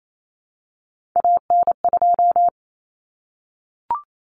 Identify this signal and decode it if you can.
Morse code, heard as a keyed tone, about 28 wpm, 700 Hz. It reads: AN2
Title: Common Cut Numbers - 3-Digit 28wpm